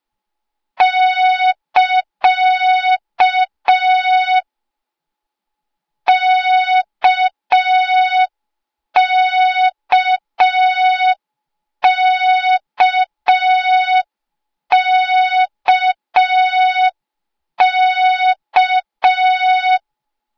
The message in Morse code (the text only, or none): <KA> KKKKK